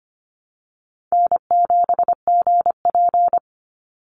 Morse code, heard as a keyed tone, about 25 wpm, 700 Hz. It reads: N7GP
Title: USA Call Signs 25wpm